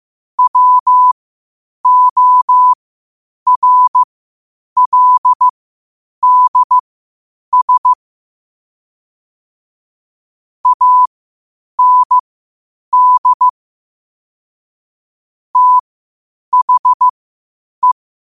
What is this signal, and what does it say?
Morse code, heard as a keyed tone, about 15 wpm, spaced as for 8 wpm, 1000 Hz. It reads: WORLDS AND THE
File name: qotdfiles / 8wpm